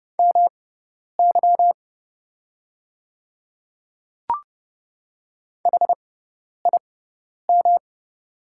Morse code, heard as a keyed tone, about 30 wpm, spaced as for 10 wpm, 690 Hz. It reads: MY HIM
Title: 6x Character Spacing - Top 100 Words - 30wpm